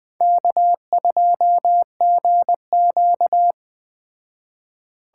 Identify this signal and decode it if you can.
Morse code, heard as a keyed tone, about 20 wpm, 700 Hz. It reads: K2GQ